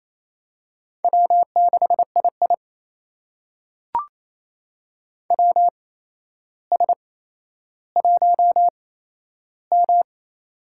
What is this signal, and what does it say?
Morse code, heard as a keyed tone, about 28 wpm, spaced as for 12 wpm, 700 Hz. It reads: W6II WS1M
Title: USA Call Signs - 4 Characters - 8x Character Spacing - 28wpm